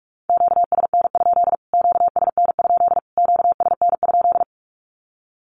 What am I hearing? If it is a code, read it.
Morse code, heard as a keyed tone, about 44 wpm, 700 Hz. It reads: QSN? QSN? QSN?
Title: Q-Codes Repeated 3x 45wpm